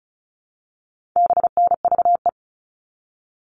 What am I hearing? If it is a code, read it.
Morse code, heard as a keyed tone, about 34 wpm, 700 Hz. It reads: BNVE
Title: All Cut Numbers - 4-Digit 35wpm